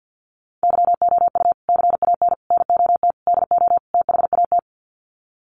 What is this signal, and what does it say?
Morse code, heard as a keyed tone, about 50 wpm, 700 Hz. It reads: YOU CAN NOT DO THAT
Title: Sets of 5 Words 50wpm